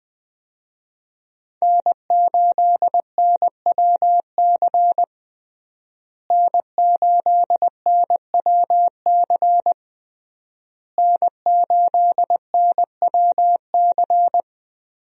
Morse code, heard as a keyed tone, about 20 wpm, 700 Hz. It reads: N8NWC N8NWC N8NWC
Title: USA Call Signs - 5 Characters - Repeated 3x - 20wpm